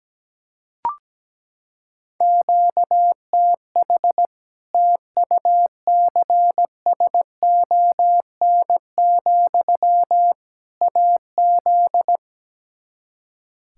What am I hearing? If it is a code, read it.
Morse code, heard as a keyed tone, about 17 wpm, 700 Hz. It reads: QTH TUCSON, AZ
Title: Practice QSO - Exchange 1 - Beginning - 17wpm